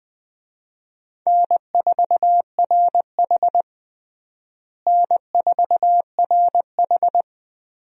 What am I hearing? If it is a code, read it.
Morse code, heard as a keyed tone, about 20 wpm, 700 Hz. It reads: N4RH N4RH
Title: USA Call Signs - 4 Characters - Repeated 2x - 20wpm